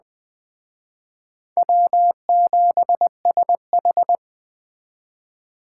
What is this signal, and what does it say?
Morse code, heard as a keyed tone, about 20 wpm, 700 Hz. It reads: W7SH
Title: Speed Racing - USA 1x2 Call Signs 17wpm